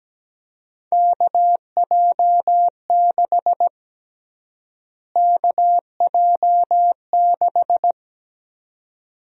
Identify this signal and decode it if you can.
Morse code, heard as a keyed tone, about 17 wpm, 700 Hz. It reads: KJ6 KJ6
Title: USA Call Sign Prefixes - Repeated 2x - 17wpm